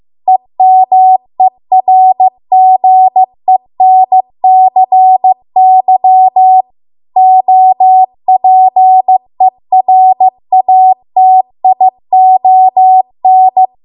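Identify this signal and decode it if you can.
Morse code, heard as a keyed tone, about 15 wpm, 750 Hz. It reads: EMERGENCY OPERATION
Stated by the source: Morse code practice at 15 wpm from the national amateur radio association's code archive.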